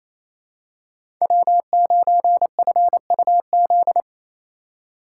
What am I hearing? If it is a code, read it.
Morse code, heard as a keyed tone, about 28 wpm, 690 Hz. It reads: W9FUZ
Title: USA Call Signs - 5 Characters - 4x Character Spacing - 28wpm